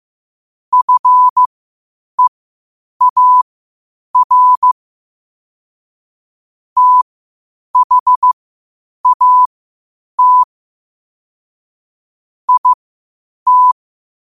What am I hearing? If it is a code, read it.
Morse code, heard as a keyed tone, about 15 wpm, spaced as for 8 wpm, 1000 Hz. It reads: FEAR THAT IT
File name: qotdfiles / 8wpm